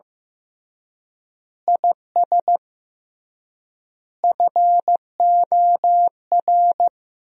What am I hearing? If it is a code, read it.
Morse code, heard as a keyed tone, about 15 wpm, 700 Hz. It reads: IS FOR
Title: 3x Word-Spacing - Sentences from Top 400 Words - 15wpm